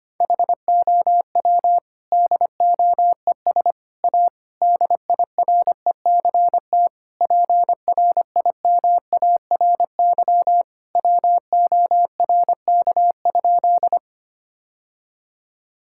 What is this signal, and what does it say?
Morse code, heard as a keyed tone, about 25 wpm, 700 Hz. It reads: HOW DOES A DIRECT PRIMARY WORK?
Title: Sentences from Top 1000 Words - Review Part 5 - 25wpm